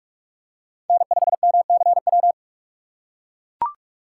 Morse code, heard as a keyed tone, about 44 wpm, 700 Hz. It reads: N5MCW